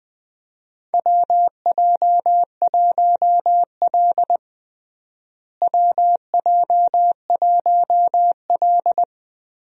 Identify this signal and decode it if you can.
Morse code, heard as a keyed tone, about 20 wpm, 700 Hz. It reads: WJ1L WJ1L
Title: USA Call Signs - 4 Characters - Repeated 2x - 20wpm